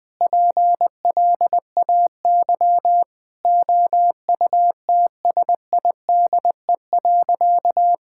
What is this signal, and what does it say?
Morse code, heard as a keyed tone, about 20 wpm, 700 Hz. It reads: PLAY OUTSIDE.
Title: Sentences with 3-Word Phrases - 20wpm